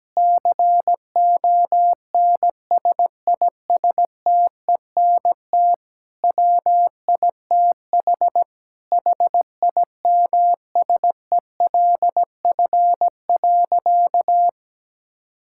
Morse code, heard as a keyed tone, about 17 wpm, 700 Hz. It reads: CONSISTENT WITH HIMSELF.
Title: Sentences from Top 1600 Words 17wpm